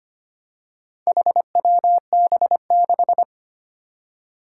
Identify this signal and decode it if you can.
Morse code, heard as a keyed tone, about 25 wpm, 700 Hz. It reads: HWB6
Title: Four Challenging Letter-Number Combinations 25wpm